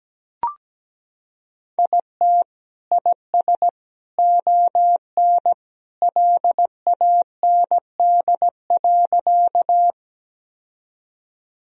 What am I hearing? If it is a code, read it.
Morse code, heard as a keyed tone, about 17 wpm, 700 Hz. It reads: IT IS ON LAND.
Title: Sentences from Top 1000 Words - Review Part 1 - 17wpm